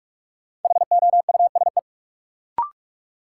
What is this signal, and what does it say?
Morse code, heard as a keyed tone, about 46 wpm, 700 Hz. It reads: HOUSE
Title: Sentences from Top 200 Words - New Word Drill - 45wpm